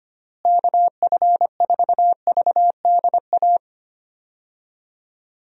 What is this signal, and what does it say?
Morse code, heard as a keyed tone, about 25 wpm, 700 Hz. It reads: KF4VDA
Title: USA Call Signs - 6 Characters - Repeated 2x - 25wpm